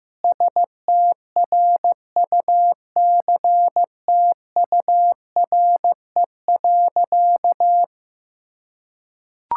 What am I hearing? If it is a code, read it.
Morse code, heard as a keyed tone, about 15 wpm, 700 Hz. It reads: STRUCTURE.
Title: Sentences from Top 1000 Words - Review Part 6 - 15wpm